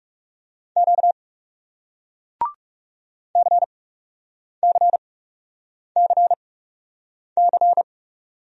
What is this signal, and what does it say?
Morse code, heard as a keyed tone, about 40 wpm, spaced as for 8 wpm, 700 Hz. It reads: Q CCCC